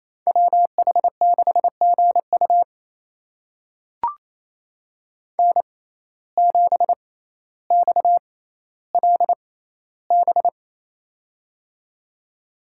Morse code, heard as a keyed tone, about 28 wpm, spaced as for 14 wpm, 700 Hz. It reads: WH6GU N7XLB
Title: USA Call Signs - 5 Characters - 6x Character Spacing - 28wpm